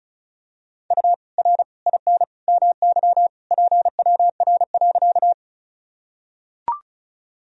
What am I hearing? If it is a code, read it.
Morse code, heard as a keyed tone, about 36 wpm, 700 Hz. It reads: U R IN MY PWR.
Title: Sentences from Top 500 Words and Abbreviations 35wpm